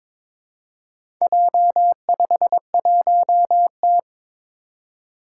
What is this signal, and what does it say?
Morse code, heard as a keyed tone, about 22 wpm, 700 Hz. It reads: J51T
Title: Four Challenging Letter-Number Combinations 22wpm